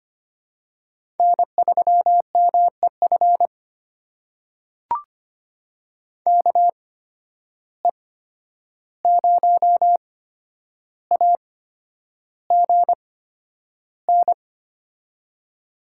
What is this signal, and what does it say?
Morse code, heard as a keyed tone, about 25 wpm, spaced as for 10 wpm, 700 Hz. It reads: N3MEF KE0AGN
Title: USA Call Signs - 8x Character Spacing - 25wpm